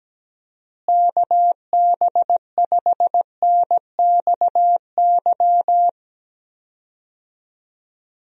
Morse code, heard as a keyed tone, about 17 wpm, 700 Hz. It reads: KB5NXY
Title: USA Call Signs - 6 Characters - Repeated 3x - 17wpm